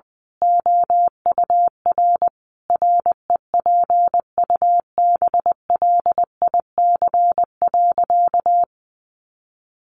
Morse code, heard as a keyed tone, about 20 wpm, 700 Hz. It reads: OUR REPUBLIC.
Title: Sentences from Top 1500 Words 20wpm